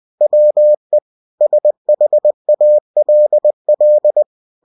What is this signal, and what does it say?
Morse code, heard as a keyed tone, about 20 wpm, 600 Hz. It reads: WE SHALL
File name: cw-books / winnie-the-pooh / winnie-20wpm-page-0085